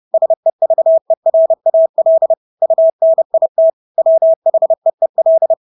Morse code, heard as a keyed tone, about 30 wpm, 650 Hz. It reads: SEVERAL UNIT WHEEL
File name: Engelske ord 30-30 wpm